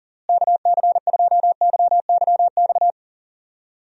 Morse code, heard as a keyed tone, about 40 wpm, 700 Hz. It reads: KC2YYX
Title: USA Call Signs - 6 Characters - Repeated 2x - 40wpm